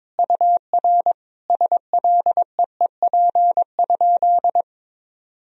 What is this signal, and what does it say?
Morse code, heard as a keyed tone, about 22 wpm, 700 Hz. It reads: UR SLEEP?
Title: Sentences from Top 500 Words and Abbreviations 22wpm